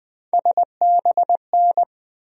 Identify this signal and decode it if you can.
Morse code, heard as a keyed tone, about 20 wpm, 700 Hz. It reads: SBN
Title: USA Call Sign Postfixes - 3 Letters 20wpm